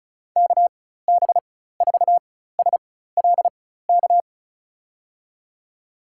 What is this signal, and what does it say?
Morse code, heard as a keyed tone, about 36 wpm, spaced as for 16 wpm, 700 Hz. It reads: KB4SLK
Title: USA Call Signs - 6 Characters - 4x Character Spacing - 35wpm